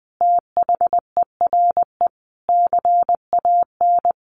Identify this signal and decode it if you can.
Morse code, heard as a keyed tone, about 19 wpm, 700 Hz. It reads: THERE CAN
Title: Sentences from Top 1400 Words 20wpm